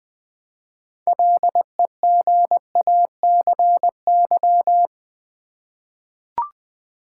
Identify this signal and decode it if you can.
Morse code, heard as a keyed tone, about 20 wpm, 700 Hz. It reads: LEGACY
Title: 6 Letter Words - First 3 Letters Given - 20wpm